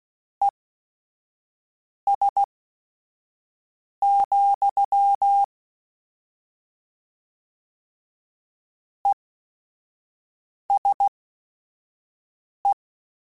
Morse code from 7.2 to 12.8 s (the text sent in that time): ESE